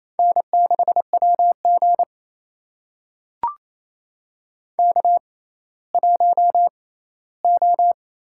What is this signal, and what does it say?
Morse code, heard as a keyed tone, about 28 wpm, spaced as for 16 wpm, 700 Hz. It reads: N6WG K1O
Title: USA Call Signs - 4 Characters - 6x Character Spacing - 28wpm